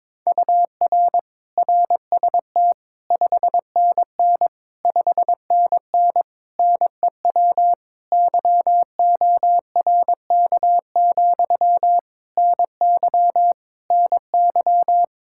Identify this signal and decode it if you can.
Morse code, heard as a keyed tone, about 22 wpm, 700 Hz. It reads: UR RST 5NN 5NN NEW YORK, NY NY
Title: Practice QSO - Exchange 1 - Beginning - 22wpm